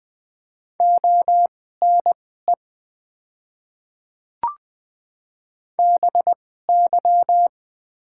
Morse code, heard as a keyed tone, about 20 wpm, spaced as for 15 wpm, 700 Hz. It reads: ONE BY
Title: 2x Character Spacing - Top 100 Words - 20wpm